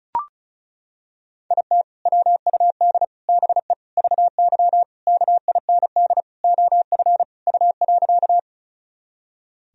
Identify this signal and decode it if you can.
Morse code, heard as a keyed tone, about 36 wpm, 700 Hz. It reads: IT WUD BE VY KIND OF U.
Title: Sentences from Top 200 Words and Abbreviations 35wpm